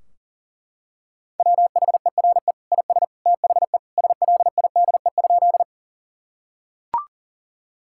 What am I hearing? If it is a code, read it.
Morse code, heard as a keyed tone, about 40 wpm, 700 Hz. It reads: WHERE IS THE SLIDE?
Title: Sentences from Top 1600 Words 40wpm